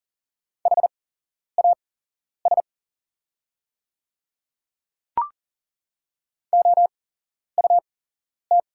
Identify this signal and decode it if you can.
Morse code, heard as a keyed tone, about 40 wpm, spaced as for 11 wpm, 700 Hz. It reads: HAS OUT
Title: 8x Character Spacing - Top 200 Words - 40wpm